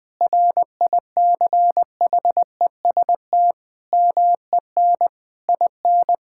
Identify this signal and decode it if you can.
Morse code, heard as a keyed tone, about 21 wpm, 700 Hz. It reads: RICHEST MEN IN
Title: Sentences with 8-Word Phrases - 20wpm